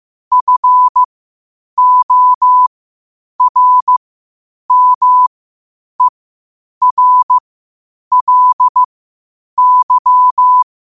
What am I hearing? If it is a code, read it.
Morse code, heard as a keyed tone, about 15 wpm, spaced as for 8 wpm, 1000 Hz. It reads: FORMERLY
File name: qotdfiles / 8wpm